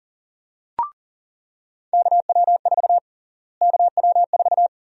Morse code, heard as a keyed tone, about 40 wpm, 700 Hz. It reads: KW4 KW4
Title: USA Call Sign Prefixes - Repeated 2x - 40wpm